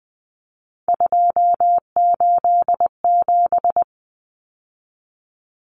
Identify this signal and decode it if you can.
Morse code, heard as a keyed tone, about 20 wpm, 700 Hz. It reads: 287
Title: Numbers - 3-Digit 20wpm